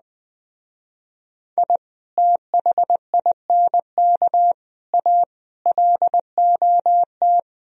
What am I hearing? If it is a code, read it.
Morse code, heard as a keyed tone, about 21 wpm, 700 Hz. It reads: I THINK A LOT